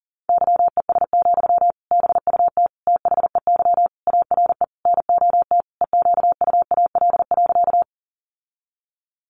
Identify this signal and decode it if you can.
Morse code, heard as a keyed tone, about 40 wpm, 700 Hz. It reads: YES, BUT THEY ARE NOT EQUAL.
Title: Sentences from Top 1400 Words 40wpm